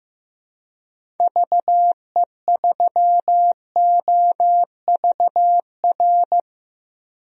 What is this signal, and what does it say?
Morse code, heard as a keyed tone, about 15 wpm, 700 Hz. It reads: VE3OVR
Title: DXCC-100 Call Signs - 6 Characters - 15wpm